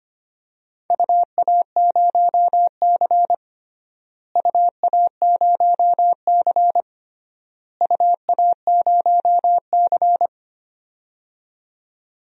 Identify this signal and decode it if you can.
Morse code, heard as a keyed tone, about 25 wpm, 700 Hz. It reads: UA0C UA0C UA0C
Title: DXCC-100 Call Signs - 4 Characters - Repeated 3x - 25wpm